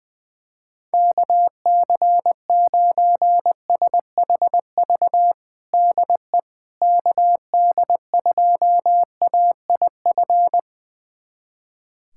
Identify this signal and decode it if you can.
Morse code, heard as a keyed tone, about 20 wpm, 700 Hz. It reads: KC9SHV DE KD2AIF
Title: USA Callsign DE Callsign - 6 Characters - 20wpm